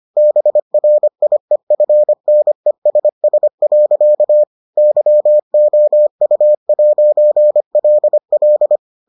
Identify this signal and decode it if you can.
Morse code, heard as a keyed tone, about 25 wpm, 600 Hz. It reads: BRIEFNESS. YOU'LL